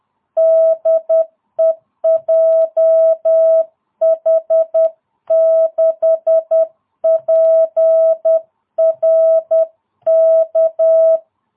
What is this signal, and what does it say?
Morse code, heard as a keyed tone, about 10 wpm, 660 Hz. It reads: DE JH6PRK